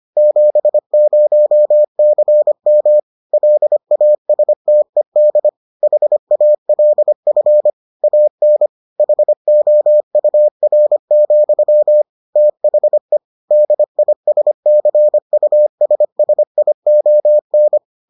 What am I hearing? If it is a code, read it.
Morse code, heard as a keyed tone, about 25 wpm, 600 Hz. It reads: 70CM LASTED HALF AN HOUR, THE DISCUSSION